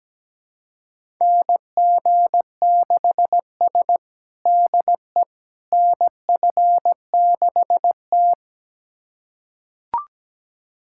Morse code, heard as a keyed tone, about 17 wpm, 700 Hz. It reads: NG6S DE NF6T
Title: USA Callsign DE Callsign - 4 Characters - 17wpm